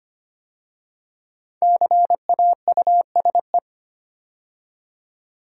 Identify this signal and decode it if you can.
Morse code, heard as a keyed tone, about 25 wpm, 700 Hz. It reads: CAUSE
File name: 5 Letter Words - First 3 Letters Given - 25wpm